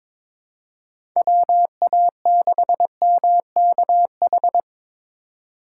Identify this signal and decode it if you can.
Morse code, heard as a keyed tone, about 23 wpm, 700 Hz. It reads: WA6MKH
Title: USA Call Signs - 6 Characters - 2x Character Spacing - 22wpm